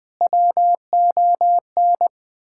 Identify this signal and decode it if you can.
Morse code, heard as a keyed tone, about 20 wpm, 700 Hz. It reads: WON
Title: Sentences with 8-Word Phrases - Spoken Beginning - 20wpm